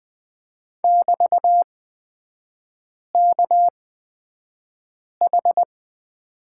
Morse code, heard as a keyed tone, about 20 wpm, spaced as for 5 wpm, 700 Hz. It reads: <BT>KH
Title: 224 Lesson 36e - BT - Three Characters 20wpm